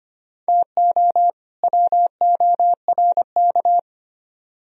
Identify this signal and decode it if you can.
Morse code, heard as a keyed tone, about 33 wpm, 700 Hz. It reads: TO WORK